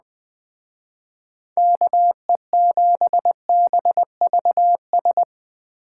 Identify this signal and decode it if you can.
Morse code, heard as a keyed tone, about 20 wpm, 700 Hz. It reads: KE7BVS